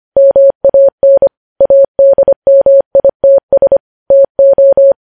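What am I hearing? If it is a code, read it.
Morse code, heard as a keyed tone, about 25 wpm, 550 Hz. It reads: MAN ADMITS TO
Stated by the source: Morse code news headlines sent at 25 wpm.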